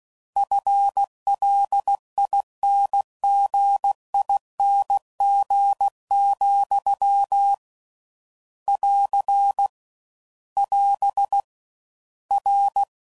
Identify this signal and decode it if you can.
Morse code, heard as a keyed tone, about 16 wpm, 800 Hz. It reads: FLINGING, <AR> <AS> R